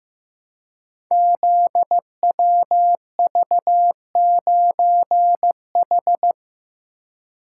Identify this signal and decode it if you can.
Morse code, heard as a keyed tone, about 15 wpm, 700 Hz. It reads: ZWV9H
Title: Code Groups - 5 Letter:Numbers - 15wpm